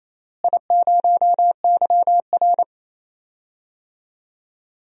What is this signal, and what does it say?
Morse code, heard as a keyed tone, about 28 wpm, 700 Hz. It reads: I0YR
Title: DXCC-100 Call Signs - 4 Characters - Repeated 3x - 28wpm